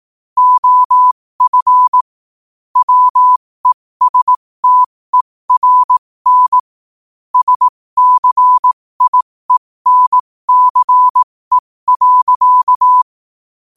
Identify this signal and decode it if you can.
Morse code, heard as a keyed tone, about 18 wpm, spaced as for 15 wpm, 1000 Hz. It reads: OF WESTERN SCIENCE.
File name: qotdfiles / 15wpm